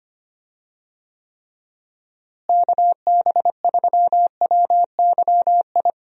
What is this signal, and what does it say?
Morse code, heard as a keyed tone, about 25 wpm, 700 Hz. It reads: KB3WYI
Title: USA Call Signs - 6 Characters - Repeated 3x - 25wpm